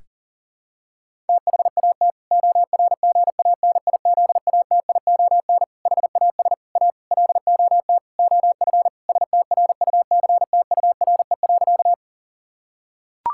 Morse code, heard as a keyed tone, about 37 wpm, 700 Hz. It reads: THAT ORGANIZATION HAS A LOT OF STRUCTURE.
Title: Sentences from Top 1000 Words - Review Part 6 - 40wpm